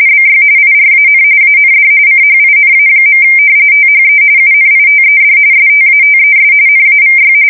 RTTY, about 45.45 baud. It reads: BY IRAN AND CHINA. THE UNITED STATES NAVY
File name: fsk170 / usnavy1